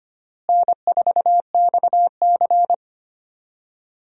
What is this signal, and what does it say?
Morse code, heard as a keyed tone, about 25 wpm, 700 Hz. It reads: N4XC
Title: USA Call Signs - 4 Characters - Repeated 3x - 25wpm